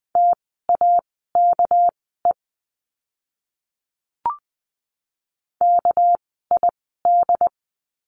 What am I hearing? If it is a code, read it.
Morse code, heard as a keyed tone, about 20 wpm, spaced as for 15 wpm, 700 Hz. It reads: TAKE KID
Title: 199 Lesson 32f - Letter k - Words 20wpm